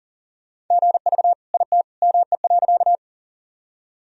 Morse code, heard as a keyed tone, about 40 wpm, 700 Hz. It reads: GV IT ME.
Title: Sentences from Top 200 Words and Abbreviations 40wpm